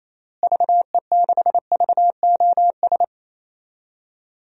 Morse code, heard as a keyed tone, about 28 wpm, 700 Hz. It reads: VE6VOS
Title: DXCC-100 Call Signs - 6 Characters - Repeated 2x - 28wpm